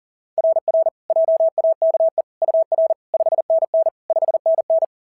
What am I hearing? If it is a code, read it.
Morse code, heard as a keyed tone, about 40 wpm, 690 Hz. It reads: RR JAKE UR 5NN 5NN
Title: SOTA QSO Practice 40wpm